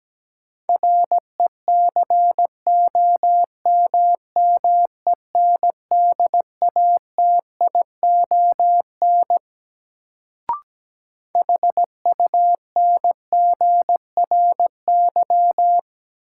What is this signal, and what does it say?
Morse code, heard as a keyed tone, about 17 wpm, 700 Hz. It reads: RECOMMENDATION HUNGRY
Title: Sentences from Top 1400 Words - New Word Drill Alt - 17wpm